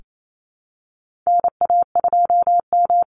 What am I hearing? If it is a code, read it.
Morse code, heard as a keyed tone, about 28 wpm, 700 Hz. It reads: NA2M
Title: Call Signs - 28wpm